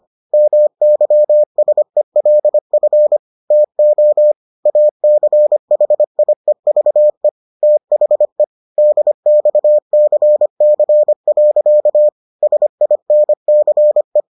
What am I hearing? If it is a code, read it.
Morse code, heard as a keyed tone, about 25 wpm, 600 Hz. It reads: MYSELF TO ACHIEVE THE DXCC. SINCE